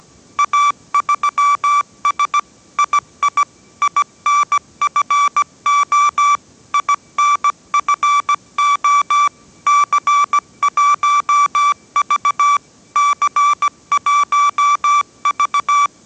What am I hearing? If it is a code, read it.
Morse code, heard as a keyed tone, about 17 wpm, 1160 Hz. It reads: A3S II INFO INFO C1V C1V